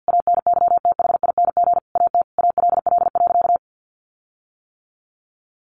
Morse code, heard as a keyed tone, about 50 wpm, 700 Hz. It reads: ANYTHING AT ALL.
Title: Sentences from Top 1300 Words 50wpm